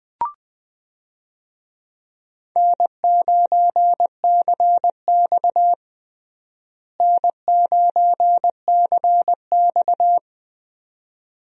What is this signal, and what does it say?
Morse code, heard as a keyed tone, about 20 wpm, 700 Hz. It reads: N9CX N9CX
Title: USA Call Signs - 4 Characters - Repeated 2x - 20wpm